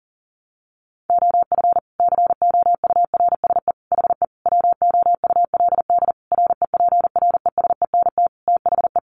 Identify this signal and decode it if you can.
Morse code, heard as a keyed tone, about 37 wpm, 700 Hz. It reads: OF COURSE HE WOULD REPRESENT THE
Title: Sentences from Top 1200 Words 40wpm